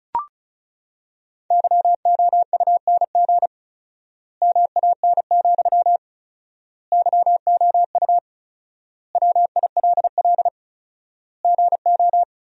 Morse code, heard as a keyed tone, about 34 wpm, 700 Hz. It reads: YOUNG MAN, YOU WILL GO
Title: 4x Word-Spacing - Sentences from Top 400 Words - 35wpm